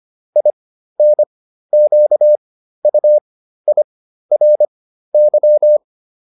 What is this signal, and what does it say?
Morse code, heard as a keyed tone, about 25 wpm, spaced as for 13 wpm, 600 Hz. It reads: INQUIRY
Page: intermediate-practice-files